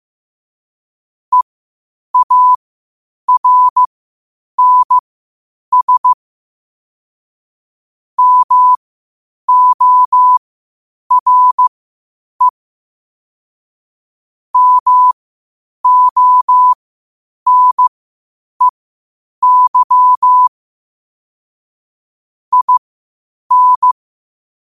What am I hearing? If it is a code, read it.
Morse code, heard as a keyed tone, about 15 wpm, spaced as for 8 wpm, 1000 Hz. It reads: EARNS MORE MONEY IN
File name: qotdfiles / 8wpm